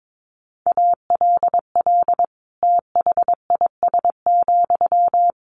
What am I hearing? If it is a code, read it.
Morse code, heard as a keyed tone, about 22 wpm, 700 Hz. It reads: ALL THIS,
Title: Sentences from Top 600 Words 22wpm